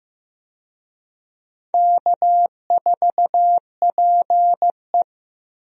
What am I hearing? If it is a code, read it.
Morse code, heard as a keyed tone, about 15 wpm, 700 Hz. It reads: K4PE
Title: USA Call Signs - 4 Characters - Repeated 3x - 15wpm